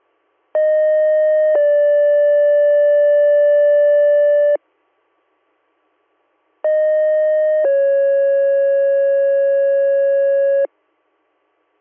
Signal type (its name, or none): two-tone page